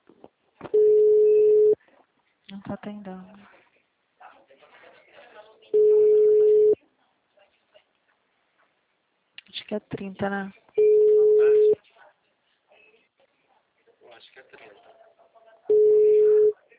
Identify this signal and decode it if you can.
Ringback tone (Europe, 425 Hz), with speech.